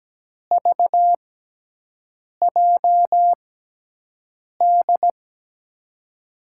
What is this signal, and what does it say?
Morse code, heard as a keyed tone, about 17 wpm, spaced as for 6 wpm, 700 Hz. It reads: VJD
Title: USA Call Signs - 6 Characters - 6x Character Spacing - 17wpm